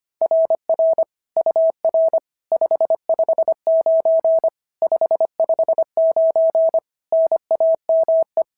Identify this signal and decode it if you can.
Morse code, heard as a keyed tone, about 24 wpm, 650 Hz. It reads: RR UR 559 559 NAME
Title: SOTA QSO Practice 25wpm